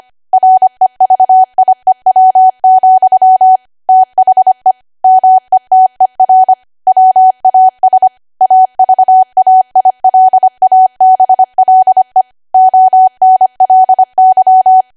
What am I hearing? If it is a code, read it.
Morse code, heard as a keyed tone, about 25 wpm, 750 Hz. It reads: REVIEW, THE METER WAS AVAILABLE ONLY
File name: cwa / W1AW-Level3 / 25WPM